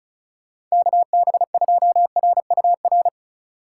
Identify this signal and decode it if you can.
Morse code, heard as a keyed tone, about 34 wpm, 700 Hz. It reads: KB2RUR
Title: USA Call Signs - 6 Characters - 3x Character Spacing - 35wpm